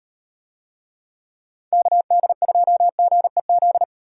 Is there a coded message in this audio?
Yes: Morse code from 1.7 to 3.8 s, heard as a keyed tone, about 36 wpm, 700 Hz. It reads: KD2GEZ